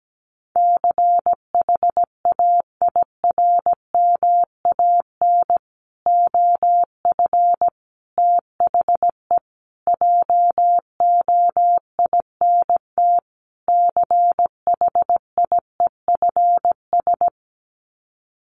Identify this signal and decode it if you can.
Morse code, heard as a keyed tone, about 17 wpm, 700 Hz. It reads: CHAIRMAN OF THE JOINT CHIEFS
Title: Sets of 5 Words Encore 17wpm